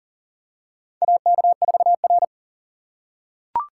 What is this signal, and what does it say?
Morse code, heard as a keyed tone, about 40 wpm, 700 Hz. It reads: AK4R
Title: USA Call Signs - 4 Characters - 8x Character Spacing - 40wpm